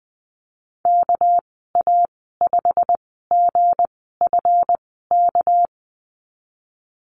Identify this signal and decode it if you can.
Morse code, heard as a keyed tone, about 19 wpm, spaced as for 14 wpm, 700 Hz. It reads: KA5GFK
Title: 245 Lesson 39g - Letter z - Call Signs 20wpm